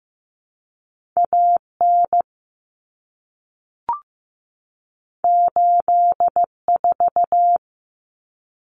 Morse code, heard as a keyed tone, about 15 wpm, 700 Hz. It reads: AN 84
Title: Common Cut Numbers - 2-Digit 15wpm